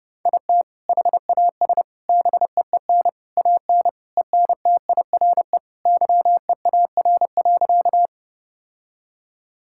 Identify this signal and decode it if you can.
Morse code, heard as a keyed tone, about 30 wpm, 700 Hz. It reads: IT HAS BEEN AN ENTIRE YEAR.
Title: Sentences from Top 1000 Words - Review Part 5 - 30wpm